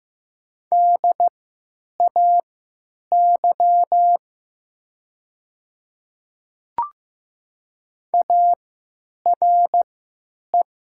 Morse code, heard as a keyed tone, about 15 wpm, spaced as for 9 wpm, 700 Hz. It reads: DAY ARE
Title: 3x Character Spacing - Top 100 Words - 15wpm